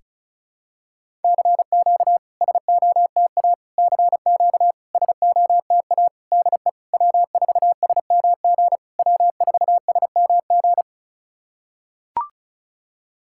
Morse code, heard as a keyed tone, about 35 wpm, 660 Hz. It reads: CQ SOTA CQ SOTA DE W4SMG W4SMG
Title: SOTA QSO Practice 35wpm